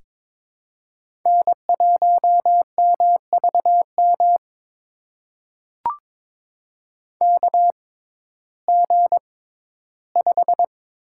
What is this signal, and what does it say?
Morse code, heard as a keyed tone, about 22 wpm, spaced as for 14 wpm, 700 Hz. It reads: N1MVM KG5
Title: USA Call Signs - 6x Character Spacing - 22wpm